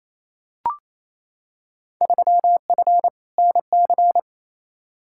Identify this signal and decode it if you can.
Morse code, heard as a keyed tone, about 28 wpm, 700 Hz. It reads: 3F NC
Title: Field Day Exchanges Encore - 28wpm